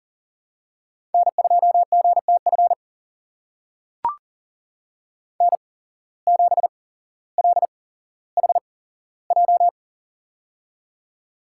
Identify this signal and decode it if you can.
Morse code, heard as a keyed tone, about 37 wpm, spaced as for 17 wpm, 700 Hz. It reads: N2GTF N7LHJ
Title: USA Call Signs - 5 Characters - 8x Character Spacing - 40wpm